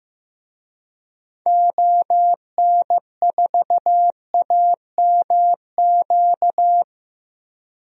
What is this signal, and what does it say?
Morse code, heard as a keyed tone, about 15 wpm, 700 Hz. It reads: ON4AMQ
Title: DXCC-100 Call Signs - 6 Characters - Repeated 2x - 15wpm